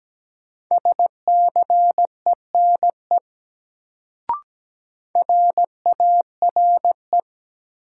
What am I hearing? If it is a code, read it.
Morse code, heard as a keyed tone, about 17 wpm, 700 Hz. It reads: SCENE RARE
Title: Sentences from Top 1100 Words - New Word Drill - 17wpm